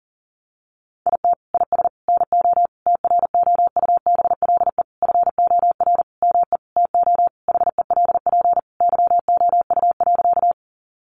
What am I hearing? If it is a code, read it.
Morse code, heard as a keyed tone, about 37 wpm, 700 Hz. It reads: IT IS NO TROUBLE FOR ME TO HELP YOU.
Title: Sentences from Top 1200 Words 40wpm